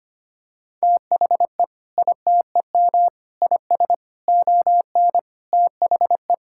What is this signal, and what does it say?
Morse code, heard as a keyed tone, about 25 wpm, 700 Hz. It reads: THE ITEM IS ON THE
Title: Sentences from Top 1000 Words - Review Part 6 - 25wpm